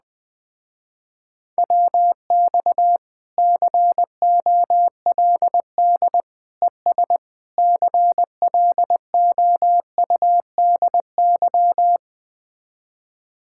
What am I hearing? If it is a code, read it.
Morse code, heard as a keyed tone, about 20 wpm, 700 Hz. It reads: WX COLD ES CLOUDY
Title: Speed-Racing Common QSO Phrases Encore - 17wpm